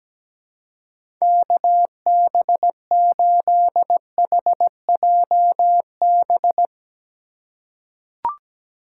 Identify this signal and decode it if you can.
Morse code, heard as a keyed tone, about 17 wpm, 700 Hz. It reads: KB8HJB
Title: USA Call Signs - 6 Characters - 2x Character Spacing - 17wpm